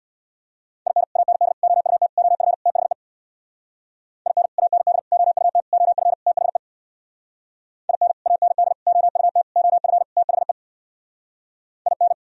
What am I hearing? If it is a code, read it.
Morse code, heard as a keyed tone, about 64 wpm, 700 Hz. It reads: IN AND OUT OF THE IN AND OUT OF THE IN AND OUT OF THE IN